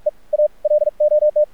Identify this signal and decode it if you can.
Morse code, heard as a keyed tone, about 46 wpm, spaced as for 32 wpm, 600 Hz. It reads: EAL OT
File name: archived / CRSnet